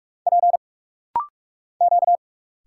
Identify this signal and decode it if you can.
Morse code, heard as a keyed tone, about 46 wpm, 700 Hz. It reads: P Q
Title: Single Letter-Number Alt - Rapid-Fire 45wpm